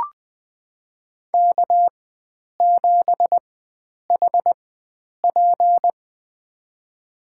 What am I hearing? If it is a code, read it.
Morse code, heard as a keyed tone, about 20 wpm, spaced as for 9 wpm, 700 Hz. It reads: K7HP
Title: USA Call Signs - 4 Characters - 4x Character Spacing - 20wpm